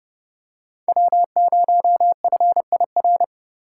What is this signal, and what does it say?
Morse code, heard as a keyed tone, about 28 wpm, 700 Hz. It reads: W0FIR